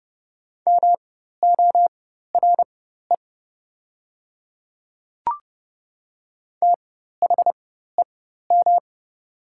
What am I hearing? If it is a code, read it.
Morse code, heard as a keyed tone, about 30 wpm, spaced as for 14 wpm, 700 Hz. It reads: MORE THEM
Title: 4x Character Spacing - Top 500 Words - 30wpm